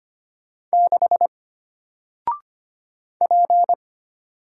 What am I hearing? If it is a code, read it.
Morse code, heard as a keyed tone, about 25 wpm, 700 Hz. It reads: 6 P
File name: Single-Letter-Number-25wpm